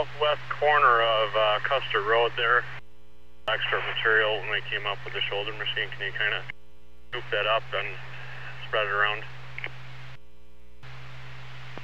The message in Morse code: none

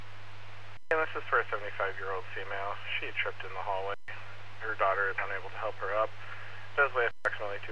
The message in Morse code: none

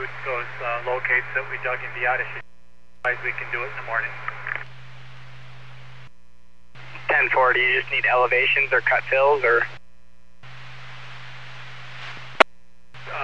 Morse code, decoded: none

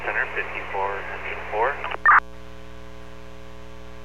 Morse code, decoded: none

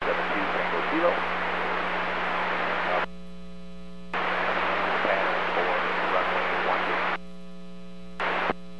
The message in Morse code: none

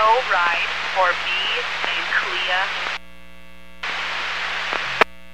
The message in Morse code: none

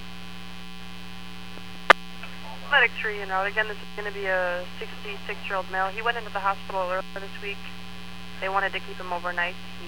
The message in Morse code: none

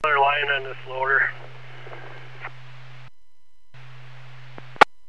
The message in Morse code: none

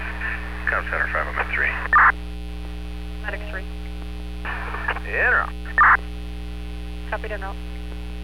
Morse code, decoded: none

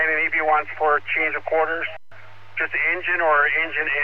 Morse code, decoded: none

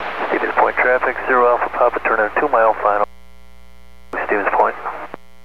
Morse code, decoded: none

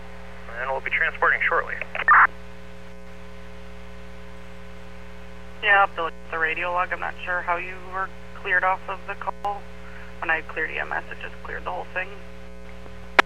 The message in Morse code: none

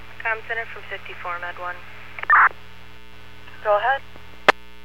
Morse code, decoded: none

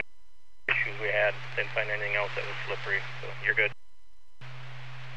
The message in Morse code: none

